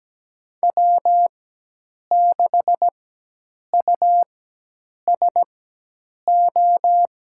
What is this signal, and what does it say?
Morse code, heard as a keyed tone, about 17 wpm, spaced as for 8 wpm, 700 Hz. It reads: W6USO
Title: USA Call Signs - 5 Characters - 4x Character Spacing - 17wpm